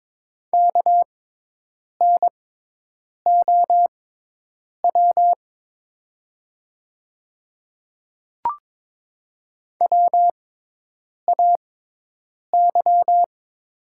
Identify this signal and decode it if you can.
Morse code, heard as a keyed tone, about 22 wpm, spaced as for 8 wpm, 700 Hz. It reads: KNOW WAY
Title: 6x Character Spacing - Top 100 Words - 22wpm